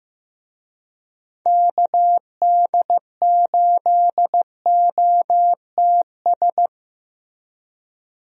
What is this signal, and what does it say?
Morse code, heard as a keyed tone, about 15 wpm, 700 Hz. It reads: KD8OTS
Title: USA Call Signs - 6 Characters - Repeated 3x - 15wpm